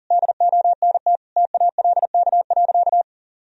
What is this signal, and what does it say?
Morse code, heard as a keyed tone, about 40 wpm, 700 Hz. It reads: DONT TALK.
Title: Sentences from Top 1000 Words - Review Part 2 - 40wpm